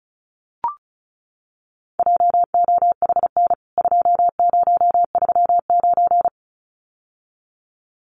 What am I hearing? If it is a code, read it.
Morse code, heard as a keyed tone, about 34 wpm, 700 Hz. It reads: JOHN 2039
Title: CWT Exchange 35wpm